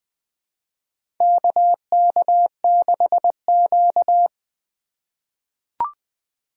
Morse code, heard as a keyed tone, about 20 wpm, 700 Hz. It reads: KK6Q
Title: USA Call Signs - 4 Characters - 4x Character Spacing - 20wpm